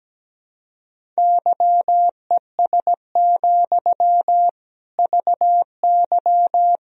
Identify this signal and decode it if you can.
Morse code, heard as a keyed tone, about 17 wpm, 700 Hz. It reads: YES, VY
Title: Sentences from Top 500 Words and Abbreviations 17wpm